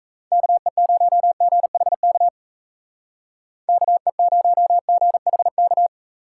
Morse code, heard as a keyed tone, about 34 wpm, 700 Hz. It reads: KE0GHK KE0GHK
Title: Speed Racing - USA Call Signs 28wpm